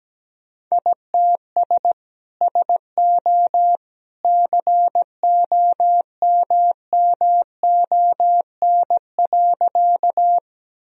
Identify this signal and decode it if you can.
Morse code, heard as a keyed tone, about 17 wpm, 700 Hz. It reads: ITS SO COMMON.
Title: Sentences from Top 1000 Words - Review Part 3 - 17wpm